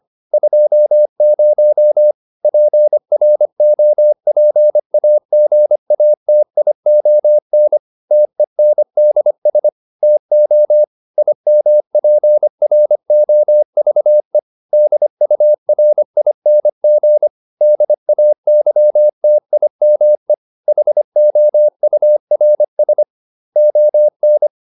Morse code, heard as a keyed tone, about 25 wpm, 600 Hz. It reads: 20 PROPAGATION TENDS TO IMPROVE DURING DAYTIME HOURS ON
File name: SS211 25